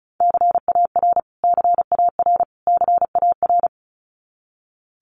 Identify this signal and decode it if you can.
Morse code, heard as a keyed tone, about 34 wpm, 700 Hz. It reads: CAR CAR CAR
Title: Top 500 Words Repeated 35wpm